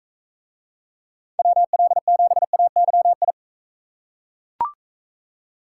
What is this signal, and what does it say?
Morse code, heard as a keyed tone, about 42 wpm, 700 Hz. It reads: WL7AYI